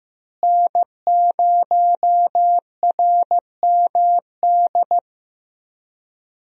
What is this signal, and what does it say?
Morse code, heard as a keyed tone, about 15 wpm, 700 Hz. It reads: N0RMD
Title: USA Call Signs - 5 Characters - Repeated 3x - 15wpm